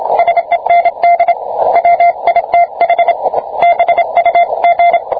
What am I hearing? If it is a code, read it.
Morse code, heard as a keyed tone, about 27 wpm, 690 Hz, with radio noise. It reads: SEND WITH BUG